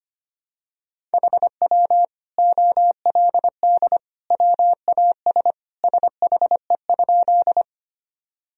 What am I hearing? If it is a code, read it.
Morse code, heard as a keyed tone, about 25 wpm, 700 Hz. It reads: HW OLD WAS SHE?
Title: Sentences from Top 200 Words and Abbreviations 25wpm